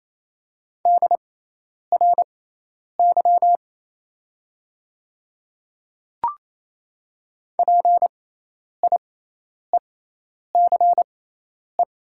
Morse code, heard as a keyed tone, about 28 wpm, spaced as for 10 wpm, 700 Hz. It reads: DRY PIECE
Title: 6x Character Spacing - Top 500 Words - 28wpm